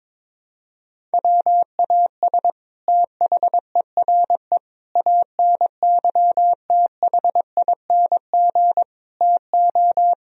Morse code, heard as a keyed tone, about 21 wpm, 700 Hz. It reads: WAS THERE ANYTHING TO
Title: Sentences from Top 1000 Words - Review Part 1 - 22wpm